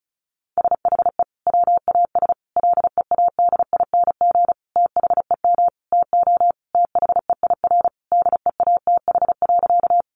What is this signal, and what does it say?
Morse code, heard as a keyed tone, about 34 wpm, 700 Hz. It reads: SHE WAS LEADING THEM TO THEIR DEATH.
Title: Sentences from Top 1100 Words 35wpm